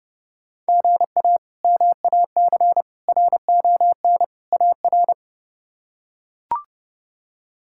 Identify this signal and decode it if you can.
Morse code, heard as a keyed tone, about 30 wpm, 700 Hz. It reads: GA MAC RON AR
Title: K1USN SST QSO Practice - 30wpm